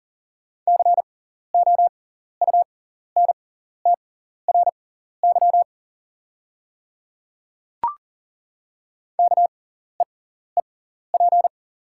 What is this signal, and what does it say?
Morse code, heard as a keyed tone, about 40 wpm, spaced as for 14 wpm, 700 Hz. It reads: COUNTRY KEEP
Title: 6x Character Spacing - Top 300 Words - 40wpm